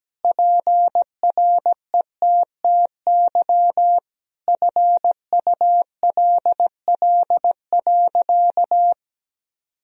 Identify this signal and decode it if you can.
Morse code, heard as a keyed tone, about 17 wpm, 700 Hz. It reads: PRETTY FULL.
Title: Sentences from Top 1000 Words - Review Part 3 - 17wpm